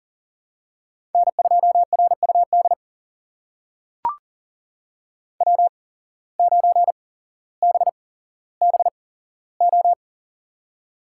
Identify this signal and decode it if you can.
Morse code, heard as a keyed tone, about 40 wpm, spaced as for 17 wpm, 700 Hz. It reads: N2RUD W9BBO
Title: USA Call Signs - 5 Characters - 8x Character Spacing - 40wpm